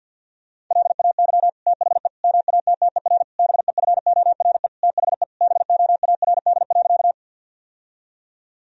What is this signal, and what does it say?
Morse code, heard as a keyed tone, about 50 wpm, 700 Hz. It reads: LAY THE MATTER BEFORE THE BOARD.
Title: Sentences from Top 1000 Words - Review Part 4 - 50wpm